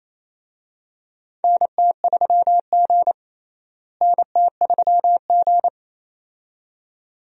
Morse code, heard as a keyed tone, about 28 wpm, 700 Hz. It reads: NT3G NT3G
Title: USA Call Signs - 4 Characters - Repeated 2x - 28wpm